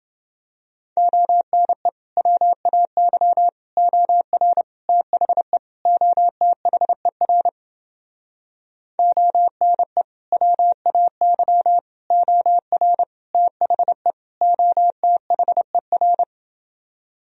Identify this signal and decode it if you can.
Morse code, heard as a keyed tone, about 30 wpm, 700 Hz. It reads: ONE WAY OR THE OTHER ONE WAY OR THE OTHER